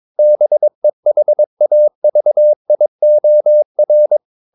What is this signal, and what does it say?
Morse code, heard as a keyed tone, about 22 wpm, 600 Hz. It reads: BEHAVIOR
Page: Romanen Never af Ken Follet på engelsk.